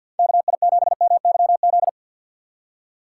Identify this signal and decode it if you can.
Morse code, heard as a keyed tone, about 48 wpm, 700 Hz. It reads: KI7MYZ
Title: USA Call Signs - 6 Characters - 6x Character Spacing - 50wpm